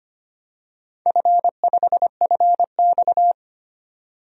Morse code, heard as a keyed tone, about 25 wpm, 700 Hz. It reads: F5FX
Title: DXCC-100 Call Signs - 4 Characters - 25wpm